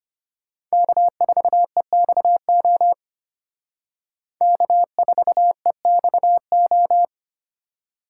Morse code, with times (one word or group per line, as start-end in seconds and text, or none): 0.7-2.9 K4EXO
4.4-7.0 K4EXO